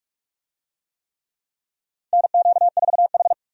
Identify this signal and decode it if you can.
Morse code, heard as a keyed tone, about 44 wpm, 700 Hz. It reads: NQ4H
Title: USA Call Signs - 4 Characters - 45wpm